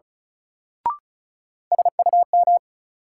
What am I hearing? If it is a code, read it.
Morse code, heard as a keyed tone, about 34 wpm, 700 Hz. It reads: SUM
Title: Sentences from Top 1000 Words - Review Part 6 - New Word Drill - 35wpm